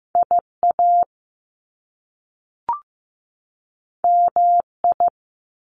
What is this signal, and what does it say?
Morse code, heard as a keyed tone, about 15 wpm, 700 Hz. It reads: IA MI
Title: Top Two Letter Combinations in Words 15wpm